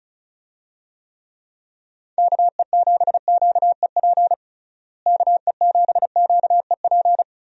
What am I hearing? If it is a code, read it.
Morse code, heard as a keyed tone, about 36 wpm, 700 Hz. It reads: KE7QEP KE7QEP
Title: USA Call Signs - 6 Characters - Repeated 3x - 35wpm